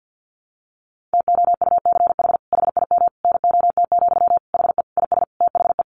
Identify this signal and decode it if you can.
Morse code, heard as a keyed tone, about 48 wpm, 700 Hz. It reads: TOUCH HIM NOT, HE IS THE
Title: Sentences from Top 1200 Words 50wpm